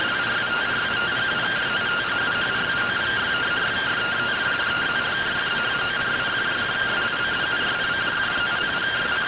Two-tone teleprinter signal, about 170 Hz shift.